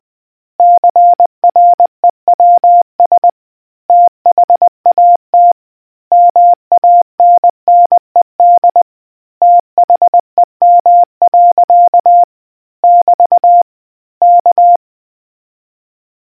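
Morse code, heard as a keyed tone, about 20 wpm, 700 Hz. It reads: CREWS THAT MANNED THEM. <BT> K
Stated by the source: Morse quiz question for naval history buffs from the RNZN Communicators Association.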